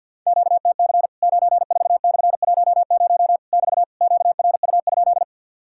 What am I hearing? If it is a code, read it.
Morse code, heard as a keyed tone, about 50 wpm, 700 Hz. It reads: QTC 94/10 <BT> QRU?
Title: Worked All Europe DX Contest - QTC Exchange - 50wpm